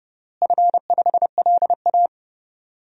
Morse code, heard as a keyed tone, about 30 wpm, 700 Hz. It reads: F5LA
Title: DXCC-100 Call Signs - 4 Characters - Repeated 3x - 30wpm